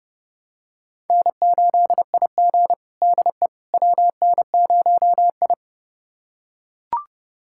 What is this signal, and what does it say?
Morse code, heard as a keyed tone, about 32 wpm, 700 Hz. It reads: N8IG DE WN0I
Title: USA Callsign DE Callsign - 4 Characters - 30wpm